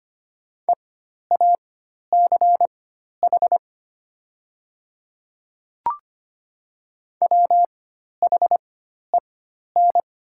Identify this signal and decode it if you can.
Morse code, heard as a keyed tone, about 25 wpm, spaced as for 12 wpm, 700 Hz. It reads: EACH WHEN
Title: 4x Character Spacing - Top 100 Words - 25wpm